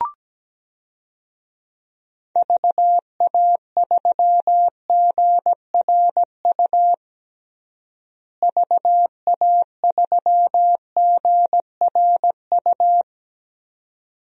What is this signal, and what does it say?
Morse code, heard as a keyed tone, about 17 wpm, 700 Hz. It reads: VA3GRU VA3GRU
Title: DXCC-100 Call Signs - 6 Characters - Repeated 2x - 17wpm